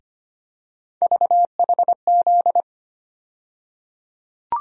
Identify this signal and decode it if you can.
Morse code, heard as a keyed tone, about 25 wpm, 700 Hz. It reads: VHZ
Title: USA Call Sign Postfixes - 3 Letters 25wpm